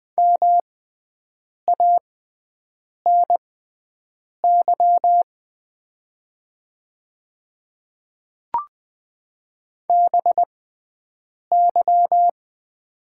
Morse code, heard as a keyed tone, about 20 wpm, spaced as for 7 wpm, 700 Hz. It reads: MANY BY
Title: 6x Character Spacing - Top 100 Words - 20wpm